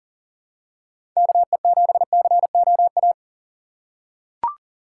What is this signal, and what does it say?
Morse code, heard as a keyed tone, about 40 wpm, 700 Hz. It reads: KE7COA
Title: USA Call Signs - 6 Characters - 6x Character Spacing - 40wpm